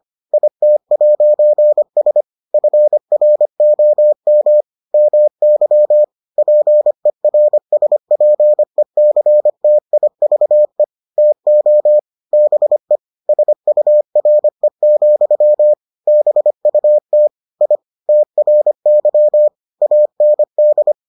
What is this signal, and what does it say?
Morse code, heard as a keyed tone, about 25 wpm, 600 Hz. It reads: IT'S FROM MY PERSPECTIVE TO BE SURE, BUT I TRY AND